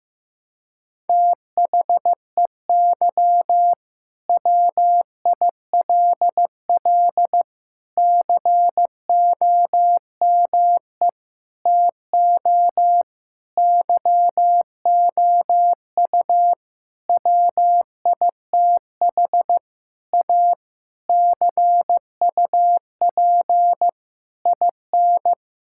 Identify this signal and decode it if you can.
Morse code, heard as a keyed tone, about 15 wpm, 700 Hz. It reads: THEY WILL COME TO YOU WITH A CUP IN
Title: Sentences from Top 1600 Words 15wpm